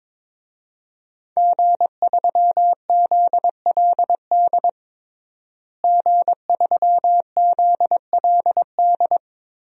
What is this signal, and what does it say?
Morse code, heard as a keyed tone, about 22 wpm, 700 Hz. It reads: G3ZLD G3ZLD
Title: DXCC-100 Call Signs - 5 Characters - Repeated 3x - 22wpm